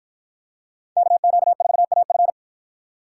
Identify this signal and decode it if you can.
Morse code, heard as a keyed tone, about 50 wpm, 700 Hz. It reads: KQ4AF